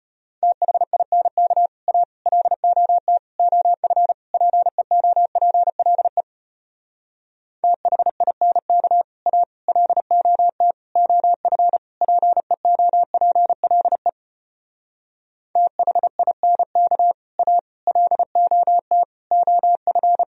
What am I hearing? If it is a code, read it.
Morse code, heard as a keyed tone, about 33 wpm, 700 Hz. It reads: THINK A LOT OF PEOPLE THINK A LOT OF PEOPLE THINK A LOT OF